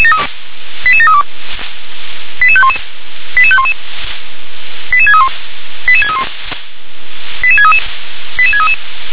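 Five-tone selective call (ZVEI), heard as quick runs of five tones, tones of about 70 ms. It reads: 88732 88511 88511 88631 88631 88633 88633
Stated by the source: Five-tone ZVEI selective call received on an SDR.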